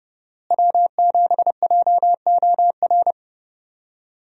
Morse code, heard as a keyed tone, about 32 wpm, 700 Hz. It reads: W7JOR
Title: USA Call Signs - 5 Characters - Repeated 2x - 30wpm